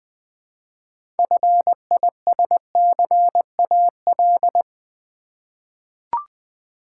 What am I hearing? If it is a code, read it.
Morse code, heard as a keyed tone, about 20 wpm, 700 Hz. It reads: FISCAL
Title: 6 Letter Words - First 3 Letters Given - 20wpm